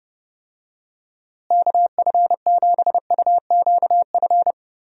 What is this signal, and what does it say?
Morse code, heard as a keyed tone, about 30 wpm, 700 Hz. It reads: KF7UQF
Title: USA Call Signs - 6 Characters - Repeated 2x - 30wpm